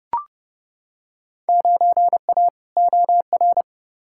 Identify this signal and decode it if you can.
Morse code, heard as a keyed tone, about 30 wpm, 700 Hz. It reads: 9A OR
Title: Field Day Exchanges Encore - 30wpm